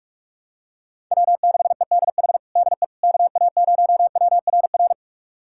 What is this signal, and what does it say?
Morse code, heard as a keyed tone, about 41 wpm, 700 Hz. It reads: W6EDH DE KA0WRR
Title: USA Callsign DE Callsign 45wpm